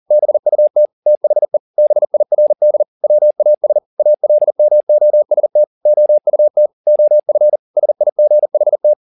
Morse code, heard as a keyed tone, about 40 wpm, 600 Hz. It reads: BUT THE BIRD WAS ALMOST OUT OF SIGHT